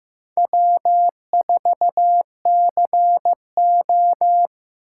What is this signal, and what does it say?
Morse code, heard as a keyed tone, about 17 wpm, 700 Hz. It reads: W4CO
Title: Speed Racing - USA 1x2 Call Signs 15wpm